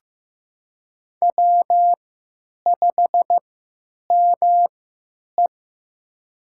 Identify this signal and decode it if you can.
Morse code, heard as a keyed tone, about 15 wpm, spaced as for 9 wpm, 700 Hz. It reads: W5ME
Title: USA Call Signs - 4 Characters - 3x Character Spacing - 15wpm